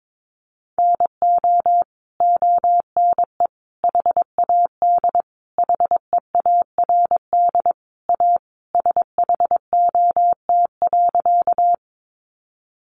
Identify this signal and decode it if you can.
Morse code, heard as a keyed tone, about 22 wpm, 700 Hz. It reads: NO ONE HAD HEARD A SHOT.
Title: Sentences from Top 1100 Words 22wpm